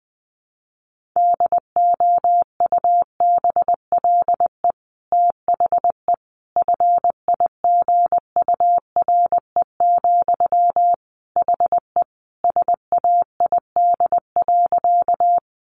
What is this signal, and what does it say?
Morse code, heard as a keyed tone, about 20 wpm, 700 Hz. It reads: DOUBLE THE FIGURE, HE SAID.
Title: Sentences from Top 1400 Words 20wpm